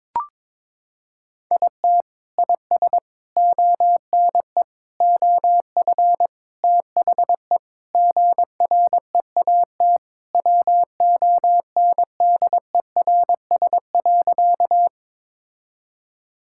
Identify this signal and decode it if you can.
Morse code, heard as a keyed tone, about 22 wpm, 700 Hz. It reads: IT IS ONE OF THE GREAT WONDERS.
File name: practice / 22wpm / Sentences with 4-Word Phrases - 22wpm